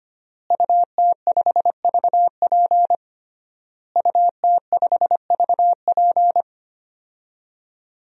Morse code, heard as a keyed tone, about 25 wpm, 700 Hz. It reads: UT5VP UT5VP
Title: DXCC-100 Call Signs - 5 Characters - Repeated 3x - 25wpm